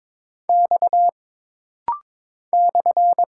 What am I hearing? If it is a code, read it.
Morse code, heard as a keyed tone, about 22 wpm, 700 Hz. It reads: X /
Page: This rapid-fire timing is geared to push your ICR (Instant Character Recognition) skills.